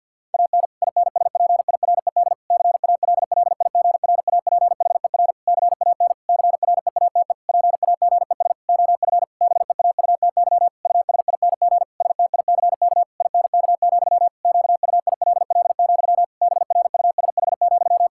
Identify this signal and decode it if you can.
Morse code, heard as a keyed tone, about 50 wpm, 700 Hz. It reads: AN INSPIRED CALLIGRAPHER CAN CREATE PAGES OF BEAUTY USING STICK INK, QUILL, BRUSH,
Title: Pangram Challenge v2 - 50wpm